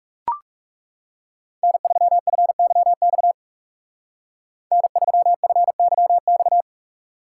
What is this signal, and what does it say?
Morse code, heard as a keyed tone, about 37 wpm, 700 Hz. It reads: N3FYX N3FYX
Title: Speed Racing - USA 1x3 Call Signs 30wpm